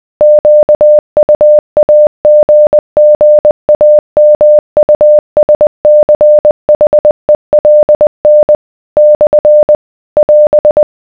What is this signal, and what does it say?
Morse code, heard as a keyed tone, about 20 wpm, 600 Hz. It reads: QUAGGAMUSCHELN / <AS>